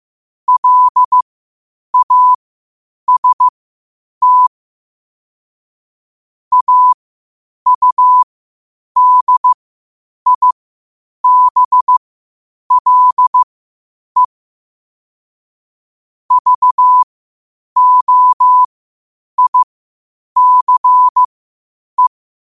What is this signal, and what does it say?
Morse code, heard as a keyed tone, about 15 wpm, spaced as for 8 wpm, 1000 Hz. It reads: LAST AUDIBLE VOICE
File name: qotdfiles / 8wpm